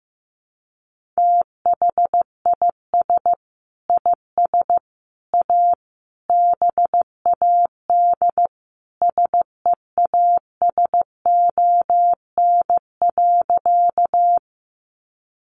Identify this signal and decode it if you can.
Morse code, heard as a keyed tone, about 15 wpm, 700 Hz. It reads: THIS IS A BAD SEASON.
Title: Sentences from Top 1400 Words 15wpm